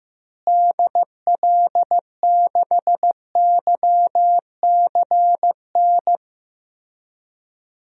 Morse code, heard as a keyed tone, about 15 wpm, 700 Hz. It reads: DL6YCN
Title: DXCC-100 Call Signs - 6 Characters - Repeated 3x - 15wpm